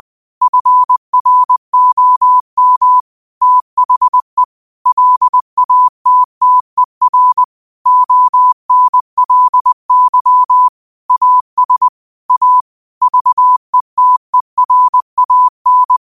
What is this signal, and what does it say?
Morse code, heard as a keyed tone, about 20 wpm, 1000 Hz. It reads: FROM THE LATTER ONLY AS A VETERAN